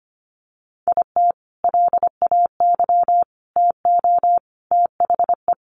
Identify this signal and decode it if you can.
Morse code, heard as a keyed tone, about 25 wpm, 700 Hz. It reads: IT LAY TO THE
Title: Sentences from Top 1400 Words 25wpm